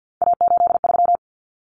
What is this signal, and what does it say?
Morse code, heard as a keyed tone, about 50 wpm, 700 Hz. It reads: A83
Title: Common Cut Numbers - 3-Digit 50wpm